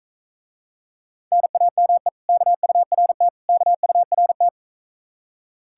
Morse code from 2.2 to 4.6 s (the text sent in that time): KURT KURT